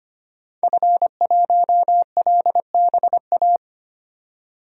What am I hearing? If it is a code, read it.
Morse code, heard as a keyed tone, about 25 wpm, 700 Hz. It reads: F1LBA
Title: DXCC-100 Call Signs - 5 Characters - 25wpm